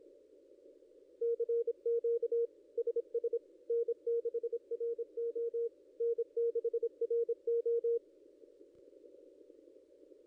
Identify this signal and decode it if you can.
Morse code, heard as a keyed tone, about 26 wpm, 470 Hz. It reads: CQ SS N6RO N6RO
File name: contests / ss-cw-k5tr-op-n5rz -0900